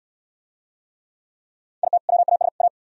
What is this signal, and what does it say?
Morse code, heard as a keyed tone, about 66 wpm, 700 Hz. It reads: IT WAS A